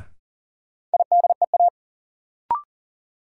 Morse code, heard as a keyed tone, about 40 wpm, 700 Hz. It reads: IDEA
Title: Sentences from Top 300 Words - New Word Drill - 40wpm